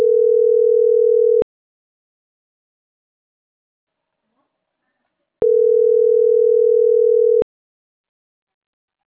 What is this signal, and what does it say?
Ringback tone (North America), 440 + 480 Hz.